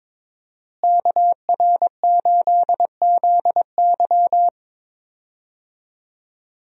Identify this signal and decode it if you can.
Morse code, heard as a keyed tone, about 22 wpm, 700 Hz. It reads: KR8ZY
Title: USA Call Signs - 5 Characters - Repeated 3x - 22wpm